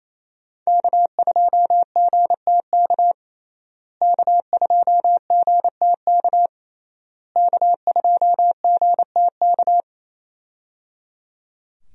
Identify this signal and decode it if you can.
Morse code, heard as a keyed tone, about 28 wpm, 700 Hz. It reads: K2GTK K2GTK K2GTK
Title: USA Call Signs - 5 Characters - Repeated 3x - 28wpm